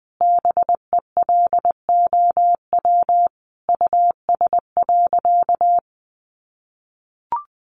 Morse code, heard as a keyed tone, about 20 wpm, 700 Hz. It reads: BELOW US.
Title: Sentences from Top 1400 Words 20wpm